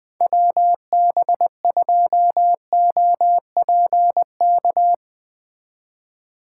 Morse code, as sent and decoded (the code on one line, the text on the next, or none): .-- -... ..--- --- .--. -.-
WB2OPK